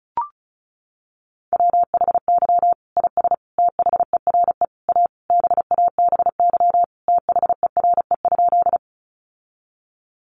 Morse code, heard as a keyed tone, about 34 wpm, 700 Hz. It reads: WHY IS THERE A BABY THERE?
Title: Sentences from Top 1300 Words 35wpm